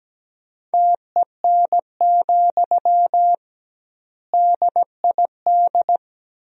Morse code, heard as a keyed tone, about 17 wpm, 700 Hz. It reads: TEN, DID
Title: 2x Word-Spacing - Sentences from Top 400 Words - 17wpm